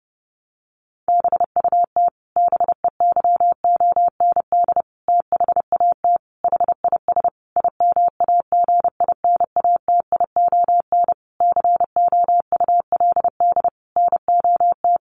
Morse code, heard as a keyed tone, about 30 wpm, 700 Hz. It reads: BUT BEYOND THAT HIS IMAGINATION COULD NOT
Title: Sentences from Top 1300 Words 30wpm